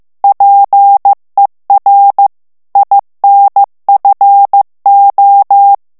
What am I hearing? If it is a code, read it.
Morse code, heard as a keyed tone, about 15 wpm, 800 Hz. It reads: PER INFO
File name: ITALIEN texte clair 1 vitesse 14 mots minute